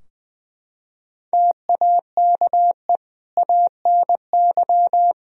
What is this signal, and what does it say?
Morse code, heard as a keyed tone, about 20 wpm, 700 Hz. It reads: TAKE ANY
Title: Sentences from Top 1000 Words - Review Part 3 - 20wpm